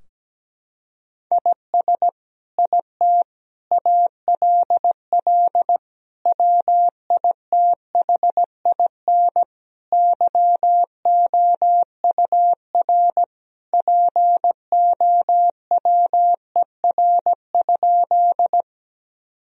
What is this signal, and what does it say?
Morse code, heard as a keyed tone, about 17 wpm, 700 Hz. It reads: IS IT ALL WITHIN YOUR POWER?
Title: Sentences from Top 1000 Words - Review Part 2 - 17wpm